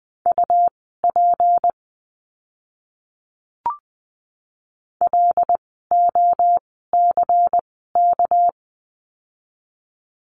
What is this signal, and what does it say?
Morse code, heard as a keyed tone, about 20 wpm, spaced as for 14 wpm, 700 Hz. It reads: UP LOCK
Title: 199 Lesson 32f - Letter k - Words 20wpm